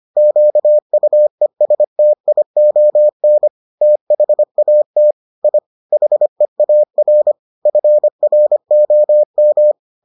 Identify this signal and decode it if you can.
Morse code, heard as a keyed tone, about 25 wpm, 600 Hz. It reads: QUESTION THAT I HEAR FROM